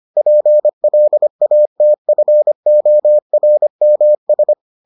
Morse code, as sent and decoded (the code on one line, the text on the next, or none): .--. .-.. .- - ..-. --- .-. -- ...
PLATFORMS